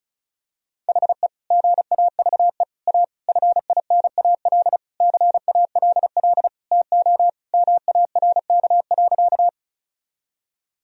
Morse code, heard as a keyed tone, about 34 wpm, 700 Hz. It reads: HE GAVE A FINAL CALL TO MARK.
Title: Sentences from Top 1000 Words - Review Part 2 - 35wpm